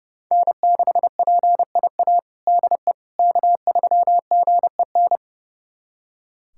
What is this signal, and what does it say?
Morse code, heard as a keyed tone, about 32 wpm, 700 Hz. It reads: N6PIA DE K3GEN
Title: USA Callsign DE Callsign - 5 Characters - 30wpm